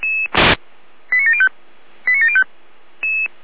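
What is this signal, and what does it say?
Five-tone selective call (ZVEI), heard as quick runs of five tones, tones of about 70 ms. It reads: 89705 89705
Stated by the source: ZVEI five-tone call received on SDR.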